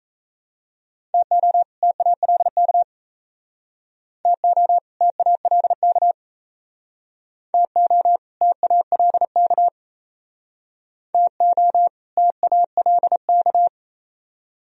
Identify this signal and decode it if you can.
Morse code, heard as a keyed tone, about 35 wpm, 700 Hz. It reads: TO TALK TO TALK TO TALK TO TALK
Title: Speed-Racing - Sets of 2 Words 28wpm